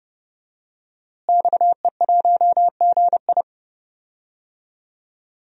Morse code, heard as a keyed tone, about 31 wpm, 700 Hz. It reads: XE1GI
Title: DXCC-100 Call Signs - 5 Characters - Repeated 2x - 30wpm